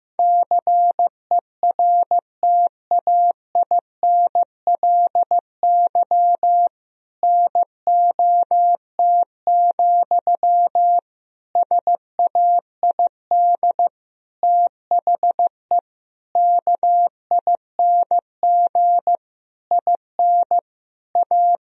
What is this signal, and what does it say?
Morse code, heard as a keyed tone, about 15 wpm, 700 Hz. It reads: CERTAINLY NOT, SAID THE KING IN A
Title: Sentences from Top 1000 Words - Review Part 1 - 15wpm